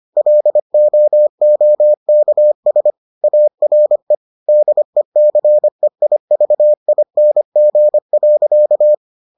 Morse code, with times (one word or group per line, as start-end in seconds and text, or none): 0.2-2.9 LOOKS
3.2-4.1 ARE
4.5-8.9 DECEIVING.